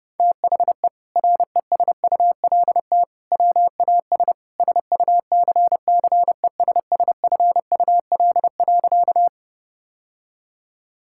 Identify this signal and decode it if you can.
Morse code, heard as a keyed tone, about 30 wpm, 700 Hz. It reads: THE RESULT WAS SUCCESSFUL.
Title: Sentences from Top 1000 Words - Review Part 6 - 30wpm